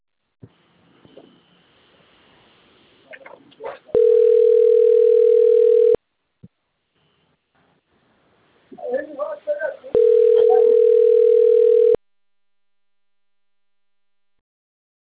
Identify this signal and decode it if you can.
Ringback tone (North America), 440 + 480 Hz, with speech.
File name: rg-1600-unknown- - - .62134